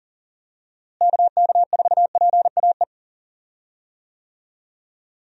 Morse code, heard as a keyed tone, about 40 wpm, 700 Hz. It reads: KK4PAE
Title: USA Call Signs - 6 Characters - Repeated 3x - 40wpm